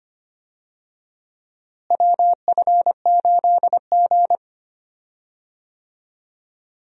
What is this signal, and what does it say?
Morse code, heard as a keyed tone, about 25 wpm, 700 Hz. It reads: WF8G
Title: USA Call Signs - 4 Characters - Repeated 3x - 25wpm